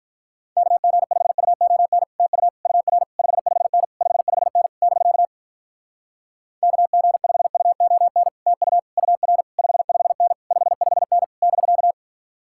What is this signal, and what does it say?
Morse code, heard as a keyed tone, about 46 wpm, 700 Hz. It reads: KG5UON TU UR 55N 55N <BK> KG5UON TU UR 55N 55N <BK>